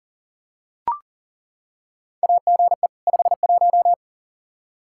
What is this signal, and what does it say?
Morse code, heard as a keyed tone, about 39 wpm, 700 Hz. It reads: AGE 51
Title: Practice QSO - Exchange 3 - More Info - 40wpm